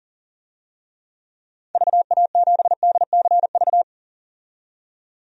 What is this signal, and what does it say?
Morse code, heard as a keyed tone, about 40 wpm, 700 Hz. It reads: VA7DCV
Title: DXCC-100 Call Signs - 6 Characters - 40wpm